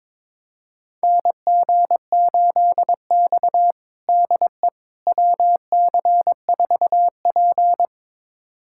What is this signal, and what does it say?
Morse code, heard as a keyed tone, about 22 wpm, 700 Hz. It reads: NG8X DE WC4P
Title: USA Callsign DE Callsign - 4 Characters - 22wpm